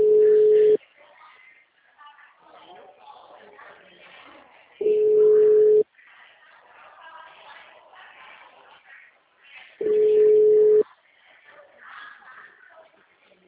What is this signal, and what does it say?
Ringback tone (Europe, 425 Hz), with speech.